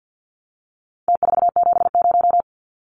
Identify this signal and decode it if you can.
Morse code, heard as a keyed tone, about 50 wpm, 700 Hz. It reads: T470
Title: Numbers - Contest Style - 4-Digit 50wpm